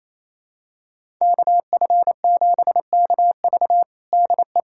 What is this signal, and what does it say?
Morse code, heard as a keyed tone, about 28 wpm, 700 Hz. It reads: KF7KV DE
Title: USA Callsign DE Callsign - 5 Characters - 28wpm